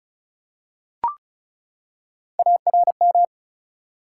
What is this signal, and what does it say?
Morse code, heard as a keyed tone, about 36 wpm, 700 Hz. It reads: ARM
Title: Three Letter Words - 35wpm